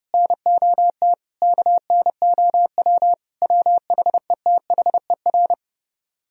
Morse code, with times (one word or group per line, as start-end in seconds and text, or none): 0.1-1.1 NOT
1.4-3.1 KNOW
3.4-5.5 WHETHER